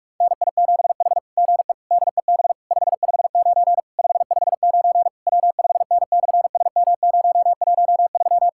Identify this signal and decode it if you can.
Morse code, heard as a keyed tone, about 44 wpm, 690 Hz. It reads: NI7H GE DEB 559 559 W5N/SM013
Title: SOTA QSO Practice 45wpm